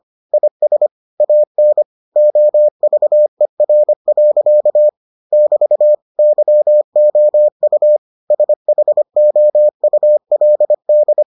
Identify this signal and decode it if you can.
Morse code, heard as a keyed tone, about 25 wpm, 600 Hz. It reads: IS AN OVER. <BT> YOU SHOULD